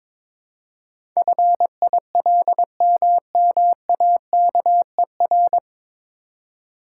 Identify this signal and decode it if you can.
Morse code, heard as a keyed tone, about 22 wpm, 700 Hz. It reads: FILMMAKER
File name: 9 Letter Words - First 3 Letters Given - 22wpm